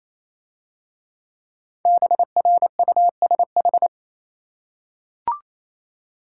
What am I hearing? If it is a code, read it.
Morse code, heard as a keyed tone, about 27 wpm, 700 Hz. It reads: BRUSH
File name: Easy Words - Speed-Racing with Hints - 22wpm